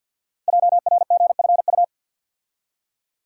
Morse code, heard as a keyed tone, about 48 wpm, 700 Hz. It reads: JRGFU
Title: Code Groups - 5 Letters - 50wpm